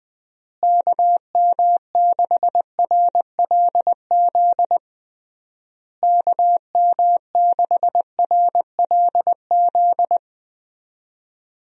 Morse code, heard as a keyed tone, about 20 wpm, 700 Hz. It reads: KM6RLZ KM6RLZ
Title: USA Call Signs - 6 Characters - Repeated 3x - 20wpm